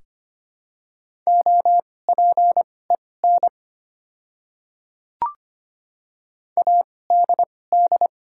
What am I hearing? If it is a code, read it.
Morse code, heard as a keyed tone, about 25 wpm, spaced as for 18 wpm, 700 Hz. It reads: OPEN ADD
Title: 2x Character Spacing - Top 300 Words - 25wpm